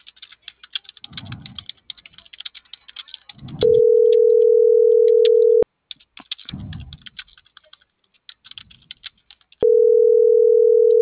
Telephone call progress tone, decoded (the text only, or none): ringback tone